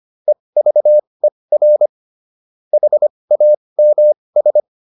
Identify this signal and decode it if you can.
Morse code, heard as a keyed tone, about 25 wpm, spaced as for 20 wpm, 600 Hz. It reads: EVER HAMS
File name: SS206 20